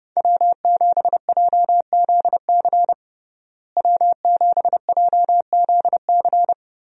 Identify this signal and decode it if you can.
Morse code, heard as a keyed tone, about 32 wpm, 700 Hz. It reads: W7JZC W7JZC
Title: USA Call Signs - 5 Characters - Repeated 2x - 30wpm